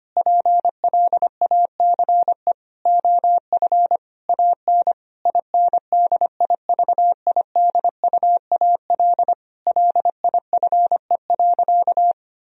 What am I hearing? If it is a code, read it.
Morse code, heard as a keyed tone, about 25 wpm, 700 Hz. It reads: PLACE OF AN INDIVIDUAL LIFE.
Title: Sentences from Top 1000 Words - Review Part 4 - 25wpm